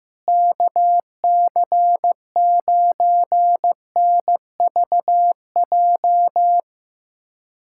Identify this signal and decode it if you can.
Morse code, heard as a keyed tone, about 15 wpm, 700 Hz. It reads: KC9NVJ
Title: USA Callsign DE Callsign - 6 Characters - 15wpm